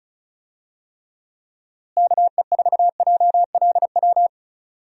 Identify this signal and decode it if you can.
Morse code, heard as a keyed tone, about 36 wpm, 700 Hz. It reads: KE4JLW
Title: USA Call Signs - 6 Characters - Repeated 2x - 35wpm